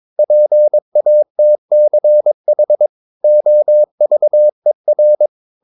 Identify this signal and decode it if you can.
Morse code, heard as a keyed tone, about 22 wpm, 600 Hz. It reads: PATCH OVER
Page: Romanen Never af Ken Follet på engelsk.